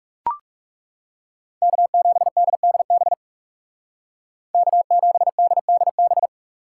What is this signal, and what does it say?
Morse code, heard as a keyed tone, about 36 wpm, 700 Hz. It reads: K7DDB K7DDB
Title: Speed Racing - USA 1x3 Call Signs 30wpm